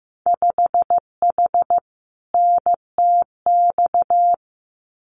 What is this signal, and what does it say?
Morse code, heard as a keyed tone, about 15 wpm, 700 Hz. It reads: 5H NTX
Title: Winter Field Day Exchanges Encore 15wpm